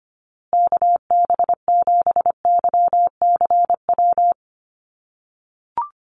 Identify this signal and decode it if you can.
Morse code, heard as a keyed tone, about 25 wpm, 700 Hz. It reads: KB7YCW
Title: USA Call Signs Repeated 2x 25wpm